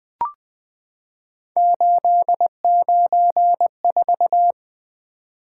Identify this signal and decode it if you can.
Morse code, heard as a keyed tone, about 20 wpm, 700 Hz. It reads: 894
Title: Three Letter-Number Combinations - 20wpm